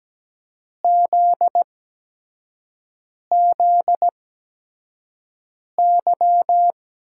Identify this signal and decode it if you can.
Morse code, heard as a keyed tone, about 17 wpm, spaced as for 5 wpm, 700 Hz. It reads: ZZY
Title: USA Call Signs - 8x Character Spacing - 17wpm